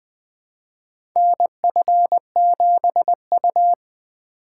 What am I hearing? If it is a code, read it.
Morse code, heard as a keyed tone, about 20 wpm, 700 Hz. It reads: NF7U
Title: USA Call Signs - 4 Characters - 2x Character Spacing - 20wpm